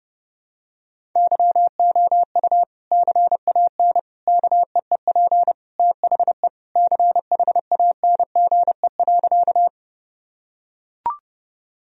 Morse code, heard as a keyed tone, about 32 wpm, 700 Hz. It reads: YOU CAN KEEP THE CHANGE.
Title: Sentences from Top 1000 Words - Review Part 1 - 30wpm